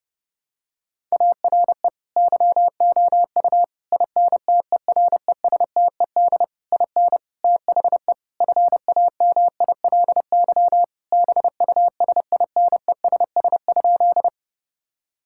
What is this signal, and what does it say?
Morse code, heard as a keyed tone, about 30 wpm, 700 Hz. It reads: ARE YOU INTERESTED IN THE FAMILY BUSINESS?
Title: Sentences from Top 1000 Words - Review Part 6 - 30wpm